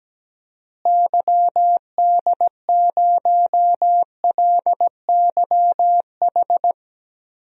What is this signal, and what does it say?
Morse code, heard as a keyed tone, about 17 wpm, 700 Hz. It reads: YD0LYH
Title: DXCC-100 Call Signs - 6 Characters - Repeated 2x - 17wpm